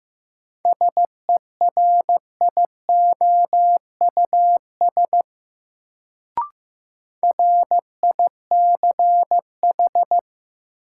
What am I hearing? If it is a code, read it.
Morse code, heard as a keyed tone, about 16 wpm, 700 Hz. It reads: SERIOUS RICH
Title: Sentences from Top 1100 Words - New Word Drill - 15wpm